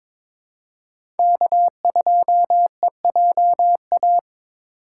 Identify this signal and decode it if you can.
Morse code, heard as a keyed tone, about 21 wpm, 700 Hz. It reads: K2EJA
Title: USA Call Signs - 5 Characters - 3x Character Spacing - 22wpm